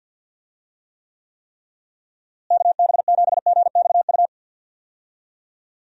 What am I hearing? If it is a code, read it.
Morse code, heard as a keyed tone, about 50 wpm, 700 Hz. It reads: KB7GXU
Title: USA Call Signs - 6 Characters - Repeated 3x - 50wpm